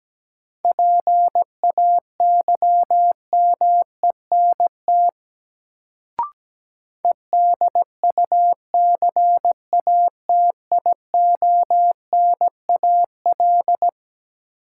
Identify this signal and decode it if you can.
Morse code, heard as a keyed tone, about 17 wpm, 700 Hz. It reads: PAYMENT EDUCATIONAL
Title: Sentences from Top 1200 Words - New Word Drill Alt - 17wpm